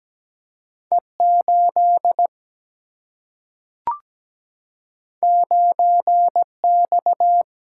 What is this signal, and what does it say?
Morse code, heard as a keyed tone, about 17 wpm, 700 Hz. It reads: E8 9X
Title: Single-Letter Single-Number 17wpm